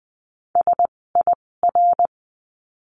Morse code, heard as a keyed tone, about 20 wpm, spaced as for 16 wpm, 700 Hz. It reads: SIR
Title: 256 Lesson 42a - Words 12-20wpm